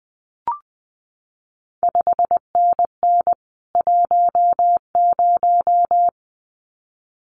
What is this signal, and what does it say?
Morse code, heard as a keyed tone, about 20 wpm, 700 Hz. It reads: 5NN 10
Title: CQ WW DX Contest Exchange Encore - 20wpm